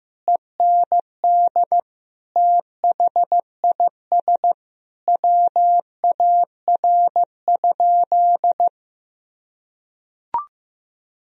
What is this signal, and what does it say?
Morse code, heard as a keyed tone, about 15 wpm, 700 Hz. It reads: END THIS WAR?
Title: Sentences from Top 1000 Words - Review Part 2 - 15wpm